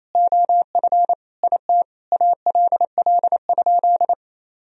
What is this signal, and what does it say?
Morse code, heard as a keyed tone, about 27 wpm, 700 Hz. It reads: OF IT ALL?
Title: Sentences from Top 1000 Words - Review Part 2 - 28wpm